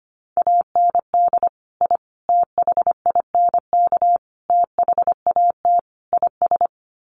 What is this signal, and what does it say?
Morse code, heard as a keyed tone, about 25 wpm, 700 Hz. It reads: AND I THINK THAT IS
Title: Sets of 5 Words Encore2 25wpm